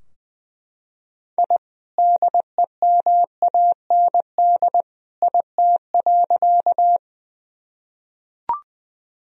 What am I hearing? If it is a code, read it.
Morse code, heard as a keyed tone, about 20 wpm, 700 Hz. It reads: I DEMAND IT.
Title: Sentences from Top 1000 Words - Review Part 4 - 20wpm